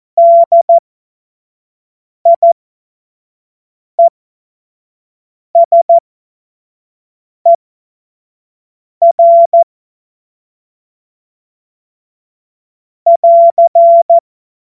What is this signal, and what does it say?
Morse code, heard as a keyed tone, about 14 wpm, spaced as for 5 wpm, 680 Hz. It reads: DIESER <AR>
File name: Morsekurs / Witze / Witz 04. Januar 2022, 25 F70